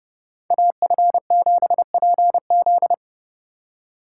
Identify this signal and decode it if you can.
Morse code, heard as a keyed tone, about 30 wpm, 700 Hz. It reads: AF7PZ